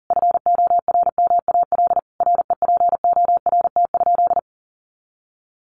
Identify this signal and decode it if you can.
Morse code, heard as a keyed tone, about 40 wpm, 700 Hz. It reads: FORMAL REPORT?
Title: Sentences from Top 1300 Words 40wpm